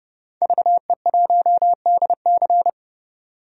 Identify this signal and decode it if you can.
Morse code, heard as a keyed tone, about 30 wpm, 700 Hz. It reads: VE1DC
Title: DXCC-100 Call Signs - 5 Characters - 30wpm